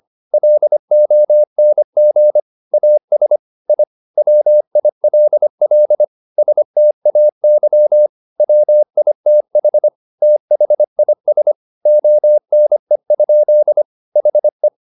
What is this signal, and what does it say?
Morse code, heard as a keyed tone, about 25 wpm, 600 Hz. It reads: LONG AS I WILL STAY WITH THIS ONE? HE
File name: the old man and the sea part 1 25wpm